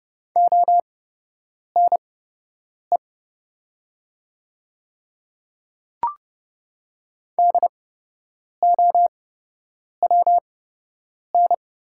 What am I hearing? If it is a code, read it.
Morse code, heard as a keyed tone, about 30 wpm, spaced as for 8 wpm, 700 Hz. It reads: ONE DOWN
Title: 8x Character Spacing - Top 100 Words - 30wpm